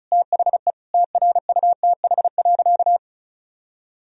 Morse code, heard as a keyed tone, about 34 wpm, 700 Hz. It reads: THE TRUTH.
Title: Sentences from Top 1000 Words - Review Part 1 - 35wpm